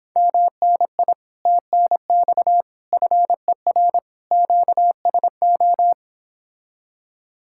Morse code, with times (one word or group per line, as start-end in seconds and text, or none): 0.2-1.1 MNI
1.5-2.6 TNX
2.9-4.0 FER
4.3-5.9 QSO